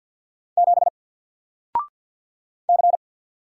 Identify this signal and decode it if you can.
Morse code, heard as a keyed tone, about 49 wpm, 700 Hz. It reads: 7 C
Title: Single Letter-Number Alt 50wpm